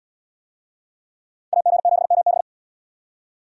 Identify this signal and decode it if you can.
Morse code, heard as a keyed tone, about 77 wpm, 700 Hz. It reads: AG7MB